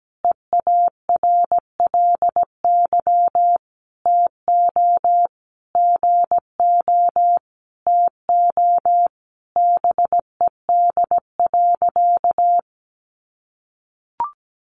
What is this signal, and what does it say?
Morse code, heard as a keyed tone, about 17 wpm, 700 Hz. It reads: EARLY TO GO TO BED.
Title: Sentences from Top 700 Words 17wpm